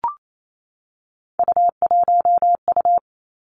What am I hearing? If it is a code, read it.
Morse code, heard as a keyed tone, about 28 wpm, 700 Hz. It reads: U1U
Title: Three Challenging Letter-Number Combinations 28wpm